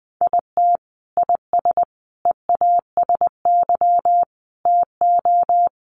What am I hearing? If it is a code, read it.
Morse code, heard as a keyed tone, about 20 wpm, 700 Hz. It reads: IT IS EASY TO
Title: Sentences from Top 1300 Words 20wpm